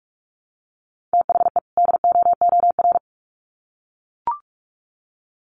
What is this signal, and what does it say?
Morse code, heard as a keyed tone, about 44 wpm, 700 Hz. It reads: THE DOOR
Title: Sets of 2 Words 45wpm